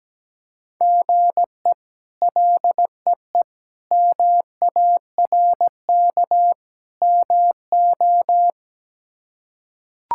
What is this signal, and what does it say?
Morse code, heard as a keyed tone, about 17 wpm, 700 Hz. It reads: GE LEE MARK MO
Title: K1USN SST QSO Practice - 17wpm